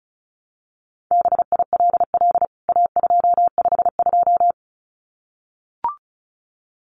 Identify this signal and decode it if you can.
Morse code, heard as a keyed tone, about 34 wpm, 700 Hz. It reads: BILL A252
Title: CWT Exchange 35wpm